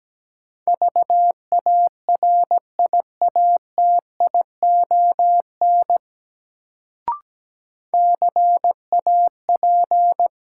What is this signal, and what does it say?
Morse code, heard as a keyed tone, about 17 wpm, 700 Hz. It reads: VARIATION CAP
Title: Sentences from Top 1600 Words - New Word Drill - 17wpm